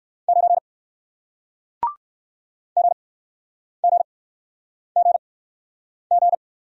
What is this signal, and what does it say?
Morse code, heard as a keyed tone, about 58 wpm, spaced as for 9 wpm, 700 Hz. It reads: 9 GGGG